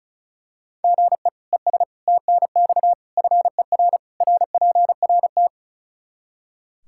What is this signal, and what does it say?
Morse code, heard as a keyed tone, about 36 wpm, 700 Hz. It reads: GE ES TNX FER RPRT
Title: K1USN SST QSO Practice 35wpm